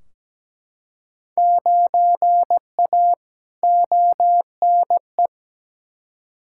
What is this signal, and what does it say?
Morse code, heard as a keyed tone, about 17 wpm, 700 Hz. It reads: 9A ONE
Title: Field Day Exchanges Encore - 17wpm